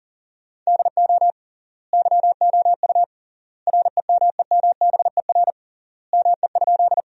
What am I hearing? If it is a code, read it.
Morse code, heard as a keyed tone, about 40 wpm, 700 Hz. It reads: DO YOU REMEMBER ME?
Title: 3x Word-Spacing - Sentences from Top 400 Words - 40wpm